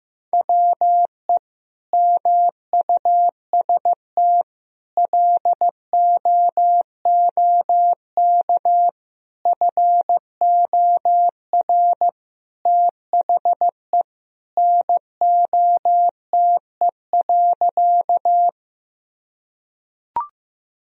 Morse code, heard as a keyed tone, about 15 wpm, 700 Hz. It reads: WE MUST LOOK FOR THE NOTE.
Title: Sentences from Top 1000 Words - Review Part 2 - 15wpm